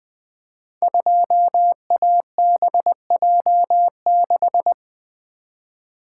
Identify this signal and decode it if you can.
Morse code, heard as a keyed tone, about 20 wpm, 700 Hz. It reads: 2ABJ6
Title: Code Groups - 5 Letter:Numbers - 20wpm